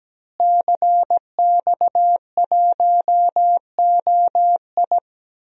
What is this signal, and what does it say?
Morse code, heard as a keyed tone, about 17 wpm, 700 Hz. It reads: CX1OI
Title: DXCC-100 Call Signs - 5 Characters - Repeated 2x - 17wpm